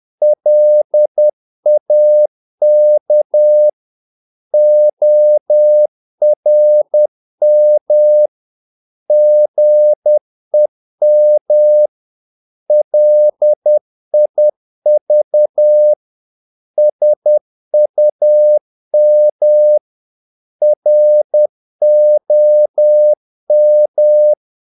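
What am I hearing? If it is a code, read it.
Morse code, heard as a keyed tone, about 10 wpm, 600 Hz. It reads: LAK ORM GEM LIV SUM ROM
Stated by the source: Morse practice of short Danish words at 10 wpm, from CW med Gnister.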